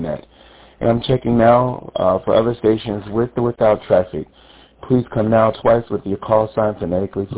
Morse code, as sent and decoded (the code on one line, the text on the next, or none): none